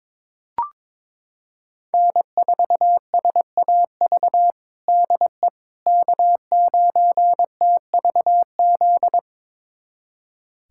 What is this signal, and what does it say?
Morse code, heard as a keyed tone, about 21 wpm, 700 Hz. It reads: N4SAV DE K9TVZ
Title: USA Callsign DE Callsign - 5 Characters - 22wpm